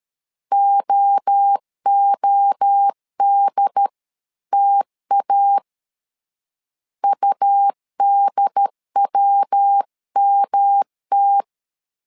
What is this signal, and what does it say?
Morse code, heard as a keyed tone, about 13 wpm, 800 Hz. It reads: OODTA UDWMT
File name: LEZIONI / 3 MWRO / 3 MWRO 12wpm 1730